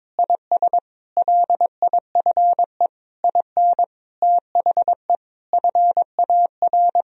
Morse code, heard as a keyed tone, about 22 wpm, 700 Hz. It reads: IS LIFE IN THE FAR
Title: Sentences from Top 1000 Words - Review Part 2 - 22wpm